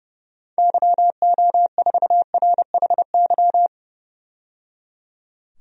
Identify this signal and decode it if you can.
Morse code, heard as a keyed tone, about 30 wpm, 700 Hz. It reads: YO4RHY
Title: DXCC-100 Call Signs - 6 Characters - 30wpm